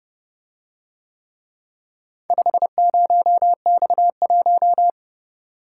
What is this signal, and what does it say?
Morse code, heard as a keyed tone, about 29 wpm, 700 Hz. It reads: 50X1